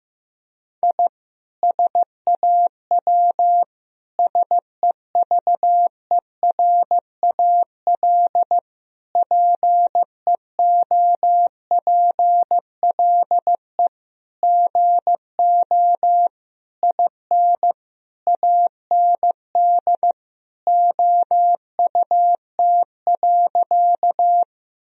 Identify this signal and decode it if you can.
Morse code, heard as a keyed tone, about 15 wpm, 700 Hz. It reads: I SAW SEVERAL PEOPLE GO IN AND OUT.
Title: Sentences from Top 1000 Words - Review Part 2 - 15wpm